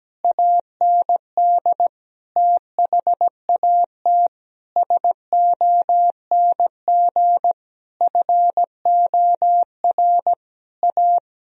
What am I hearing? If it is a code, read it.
Morse code, heard as a keyed tone, about 17 wpm, 700 Hz. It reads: AND THAT SONG FOR A
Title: Sentences from Top 1000 Words - Review Part 2 - 17wpm